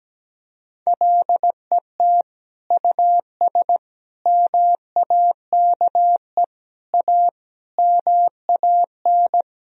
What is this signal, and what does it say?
Morse code, heard as a keyed tone, about 17 wpm, 700 Hz. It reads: LET US MAKE A MAN
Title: Sentences from Top 1000 Words - Review Part 4 - 17wpm